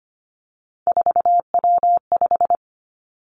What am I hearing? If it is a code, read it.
Morse code, heard as a keyed tone, about 25 wpm, 700 Hz. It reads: 4W5
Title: Three Challenging Letter-Number Combinations 25wpm